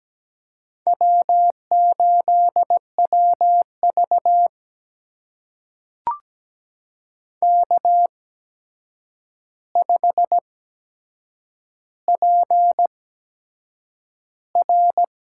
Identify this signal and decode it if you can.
Morse code, heard as a keyed tone, about 17 wpm, spaced as for 7 wpm, 700 Hz. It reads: W8WV K5PR
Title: USA Call Signs - 8x Character Spacing - 17wpm